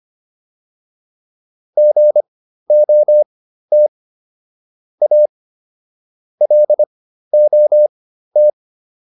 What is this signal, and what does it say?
Morse code, heard as a keyed tone, about 25 wpm, spaced as for 13 wpm, 600 Hz. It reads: GOT A LOT